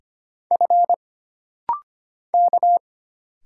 Morse code, heard as a keyed tone, about 25 wpm, 700 Hz. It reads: F K
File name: Single Character (Letter:Number:Punctuation:Prosign) - Rapid-Fire - 25wpm